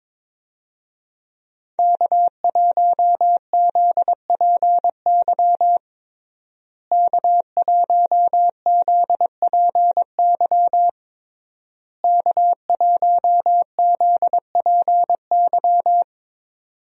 Morse code, heard as a keyed tone, about 22 wpm, 700 Hz. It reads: K1ZPY K1ZPY K1ZPY
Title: USA Call Signs - 5 Characters - Repeated 3x - 22wpm